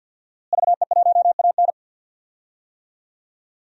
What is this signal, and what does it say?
Morse code, heard as a keyed tone, about 52 wpm, 700 Hz. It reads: VE1AN
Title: DXCC-100 Call Signs - 5 Characters - Repeated 2x - 50wpm